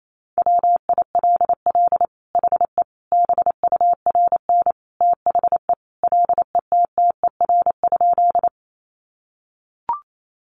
Morse code, heard as a keyed tone, about 28 wpm, 700 Hz. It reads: WILL HE BURN THE LETTER?
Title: Sentences from Top 1300 Words 28wpm